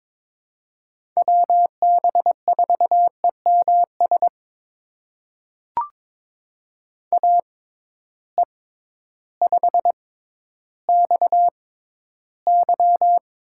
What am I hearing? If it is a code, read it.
Morse code, heard as a keyed tone, about 22 wpm, spaced as for 12 wpm, 700 Hz. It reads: WB4EMS AE5XY
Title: USA Call Signs - 6x Character Spacing - 22wpm